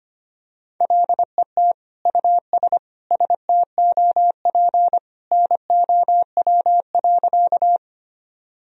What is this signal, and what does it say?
Morse code, heard as a keyed tone, about 25 wpm, 700 Hz. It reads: LET US STOP NOW.
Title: Sentences from Top 1000 Words - Review Part 1 - 25wpm